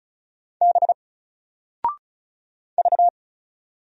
Morse code, heard as a keyed tone, about 36 wpm, 700 Hz. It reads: B V
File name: Single-Letter-Number-35wpm